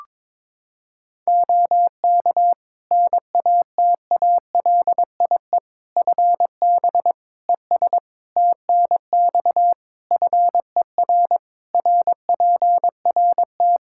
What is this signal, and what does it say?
Morse code, heard as a keyed tone, about 22 wpm, 700 Hz. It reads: OK NATALIE FB ES TNX FER RPRT
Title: Practice QSO - Exchange 2 - Info - 22wpm